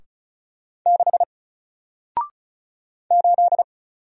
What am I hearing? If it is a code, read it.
Morse code, heard as a keyed tone, about 36 wpm, 700 Hz. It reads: 6 8
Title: Single Letter-Number Alt 35wpm